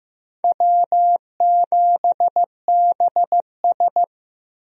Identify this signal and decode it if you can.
Morse code, heard as a keyed tone, about 15 wpm, 700 Hz. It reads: W7BS
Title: USA Call Signs - 4 Characters - Repeated 2x - 15wpm